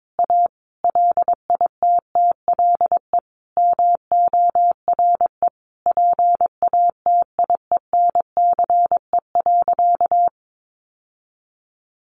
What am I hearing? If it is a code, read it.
Morse code, heard as a keyed tone, about 22 wpm, 700 Hz. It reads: A LITTLE MORE PATIENCE.
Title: Sentences from Top 1300 Words 22wpm